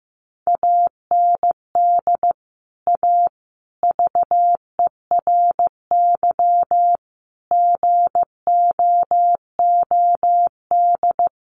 Morse code, heard as a keyed tone, about 15 wpm, 700 Hz. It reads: AND A VERY GOOD
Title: Sentences from Top 200 Words 15wpm